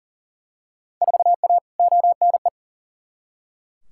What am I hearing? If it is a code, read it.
Morse code, heard as a keyed tone, about 37 wpm, 700 Hz. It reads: 4A ONE
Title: Field Day Exchanges Encore - 40wpm